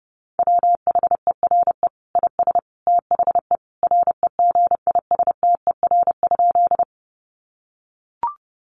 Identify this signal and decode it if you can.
Morse code, heard as a keyed tone, about 30 wpm, 700 Hz. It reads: WHERE IS THE REGISTER?
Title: Sentences from Top 1200 Words 30wpm